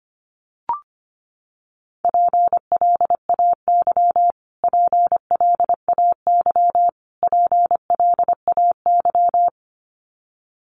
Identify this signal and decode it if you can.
Morse code, heard as a keyed tone, about 25 wpm, 700 Hz. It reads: PLAY PLAY PLAY
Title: Top 500 Words Repeated 25wpm